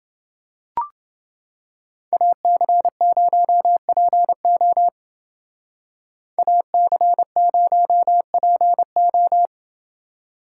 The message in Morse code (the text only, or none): AC0PO AC0PO